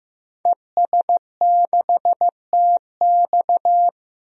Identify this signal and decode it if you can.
Morse code, heard as a keyed tone, about 15 wpm, 700 Hz. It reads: ES6TX
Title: DXCC-100 Call Signs - 5 Characters - Repeated 2x - 15wpm